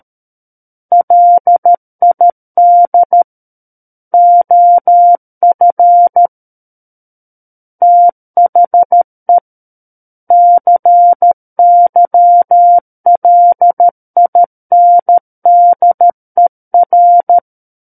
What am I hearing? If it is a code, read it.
Morse code, heard as a keyed tone, about 13 wpm, 700 Hz. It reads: LID OF THE CYLINDER
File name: Book One / 04-Chapter 4 - 13 WPM